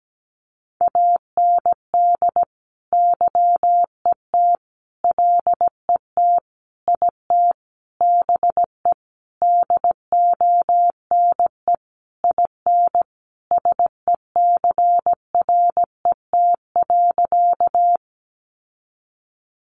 Morse code, heard as a keyed tone, about 17 wpm, 700 Hz. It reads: AND YET LET IT BE DONE IN SECRET.
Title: Sentences from Top 1200 Words 17wpm